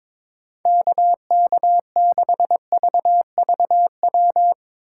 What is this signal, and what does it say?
Morse code, heard as a keyed tone, about 22 wpm, 700 Hz. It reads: KK6VVW
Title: USA Call Signs - 6 Characters - 4x Character Spacing - 22wpm